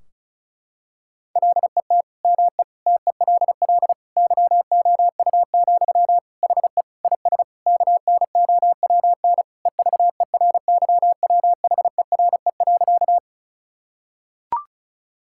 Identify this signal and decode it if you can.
Morse code, heard as a keyed tone, about 36 wpm, 700 Hz. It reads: LET ME TELL YOU, HE IS KNOWN EVERYWHERE.
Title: Sentences from Top 1600 Words 35wpm